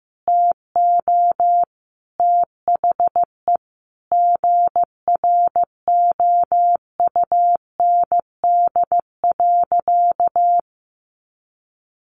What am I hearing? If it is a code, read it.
Morse code, heard as a keyed tone, about 15 wpm, 700 Hz. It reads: TO THE GROUND.
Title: Sentences from Top 500 Words 15wpm